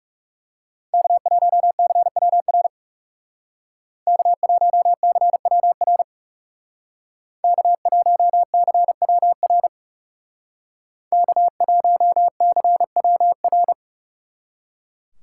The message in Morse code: K1CWR K1CWR K1CWR K1CWR

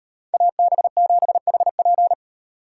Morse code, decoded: AB7HP

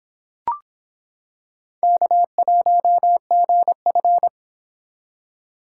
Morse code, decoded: K1GF

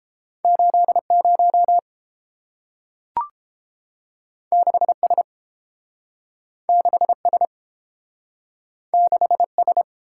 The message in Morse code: 80 6S 6S 6S

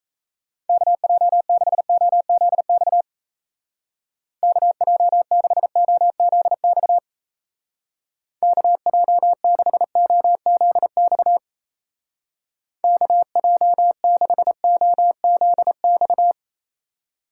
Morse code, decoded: KJ6OZX KJ6OZX KJ6OZX KJ6OZX

